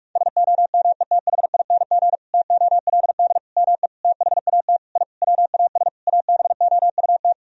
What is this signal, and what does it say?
Morse code, heard as a keyed tone, about 41 wpm, 700 Hz. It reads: SOMETHING TOLD ME THAT I WAS ABOUT
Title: Sentences from Top 1000 Words - Review Part 1 - 45wpm